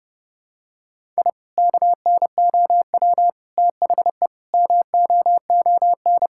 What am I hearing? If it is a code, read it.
Morse code, heard as a keyed tone, about 30 wpm, 700 Hz. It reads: I KNOW THE MOON
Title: Sentences from Top 1000 Words - Review Part 6 - 30wpm